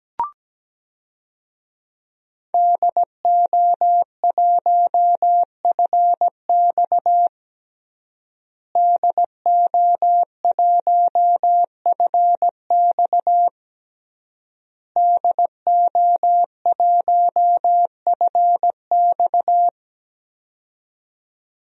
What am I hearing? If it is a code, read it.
Morse code, heard as a keyed tone, about 17 wpm, 700 Hz. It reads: DO1FX DO1FX DO1FX
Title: DXCC-100 Call Signs - 5 Characters - Repeated 3x - 17wpm